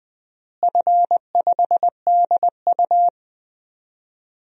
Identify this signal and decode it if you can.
Morse code, heard as a keyed tone, about 21 wpm, 700 Hz. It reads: F5DU
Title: DXCC-100 Call Signs - 4 Characters - Repeated 2x - 20wpm